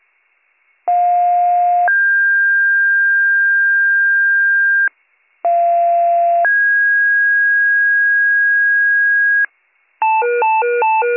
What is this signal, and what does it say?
Two-tone page, heard as a one-second tone, then a three-second tone, about 710 then 1600 Hz, about 690 then 1650 Hz. It is heard from 0.9 to 9.5 s.